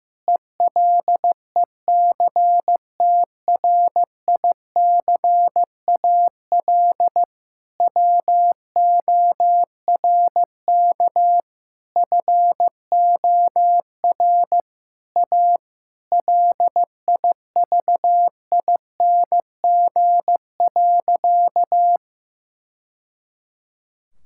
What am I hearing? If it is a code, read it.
Morse code, heard as a keyed tone, about 16 wpm, 700 Hz. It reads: ELECTRICAL WORK FOR A LIVING.
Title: Sentences from Top 1000 Words - Review Part 6 - 15wpm